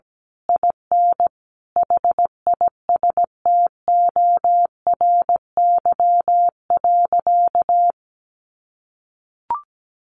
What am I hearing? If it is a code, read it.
Morse code, heard as a keyed tone, about 17 wpm, 700 Hz. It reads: IN HISTORY.
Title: A Sound Of Thunder 17wpm